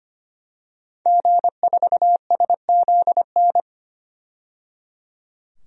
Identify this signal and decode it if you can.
Morse code, heard as a keyed tone, about 25 wpm, 700 Hz. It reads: G4SZN
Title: DXCC-100 Call Signs - 5 Characters - 25wpm